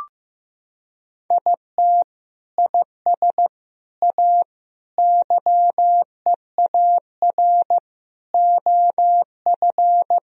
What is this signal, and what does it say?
Morse code, heard as a keyed tone, about 15 wpm, 700 Hz. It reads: IT IS A YEAR OF
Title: Sentences from Top 1000 Words - Review Part 1 - 15wpm